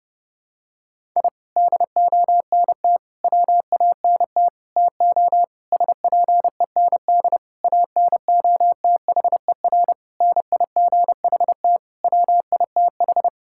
Morse code, heard as a keyed tone, about 30 wpm, 700 Hz. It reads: I DONT WANT TO SPEND ANOTHER NIGHT WITH
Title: Sentences from Top 1000 Words - Review Part 5 - 30wpm